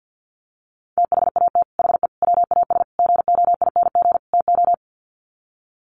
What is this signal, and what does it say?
Morse code, heard as a keyed tone, about 48 wpm, 700 Hz. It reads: THAT HE WAS GOING TO
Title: Sets of 5 Words 50wpm